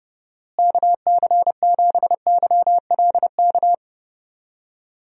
Morse code, heard as a keyed tone, about 30 wpm, 700 Hz. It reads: KC7YLK
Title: USA Call Signs - 6 Characters - Repeated 2x - 30wpm